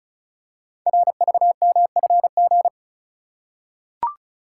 Morse code, heard as a keyed tone, about 36 wpm, 700 Hz. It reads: RVMFG
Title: Code Groups - 5 Letter:Numbers - 35wpm